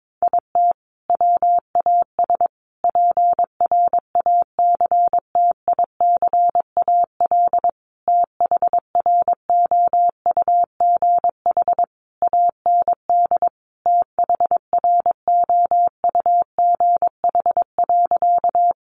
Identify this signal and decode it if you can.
Morse code, heard as a keyed tone, about 22 wpm, 700 Hz. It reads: IT WAS PRACTICAL THROUGH AND THROUGH.
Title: Sentences from Top 1400 Words 22wpm